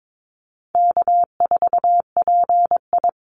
Morse code, heard as a keyed tone, about 22 wpm, 700 Hz. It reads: K4PI
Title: Call Signs Repeated 3x - 22wpm